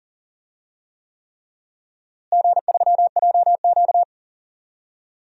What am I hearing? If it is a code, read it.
Morse code, heard as a keyed tone, about 40 wpm, 700 Hz. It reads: G3JQ
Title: DXCC-100 Call Signs - 4 Characters - Repeated 3x - 40wpm